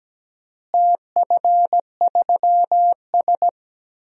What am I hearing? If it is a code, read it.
Morse code, heard as a keyed tone, about 17 wpm, 700 Hz. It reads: TF3S
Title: DXCC-100 Call Signs - 4 Characters - Repeated 2x - 17wpm